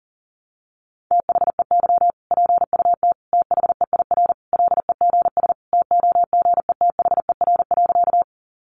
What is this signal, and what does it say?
Morse code, heard as a keyed tone, about 40 wpm, 700 Hz. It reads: THEY PUT THEIR LEGS TOGETHER.
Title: Easy Phrases 40wpm